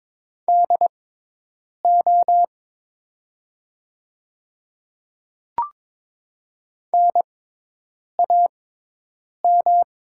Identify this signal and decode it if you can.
Morse code, heard as a keyed tone, about 22 wpm, spaced as for 8 wpm, 700 Hz. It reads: DO NAM
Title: 6x Character Spacing - Top 200 Words - 22wpm